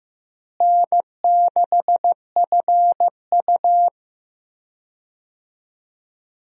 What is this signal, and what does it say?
Morse code, heard as a keyed tone, about 15 wpm, 700 Hz. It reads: N6FU
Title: USA Call Signs - 4 Characters - Repeated 3x - 15wpm